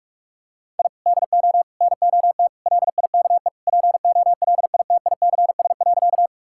Morse code, heard as a keyed tone, about 41 wpm, 700 Hz. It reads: I DO NOT LIKE POLITICS.
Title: Sentences from Top 1000 Words - Review Part 6 - 45wpm